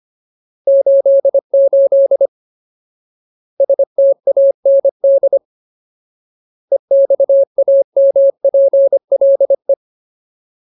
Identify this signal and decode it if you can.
Morse code, heard as a keyed tone, about 25 wpm, 550 Hz. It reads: 88 STAND EXAMPLE
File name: Ultimate Word List / 25WPM Ultimate